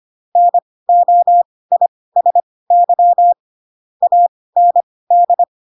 Morse code, heard as a keyed tone, about 25 wpm, spaced as for 18 wpm, 700 Hz. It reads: NOISY AND
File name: Pinocchio / Pinocchio ch7 12 25 ik1hge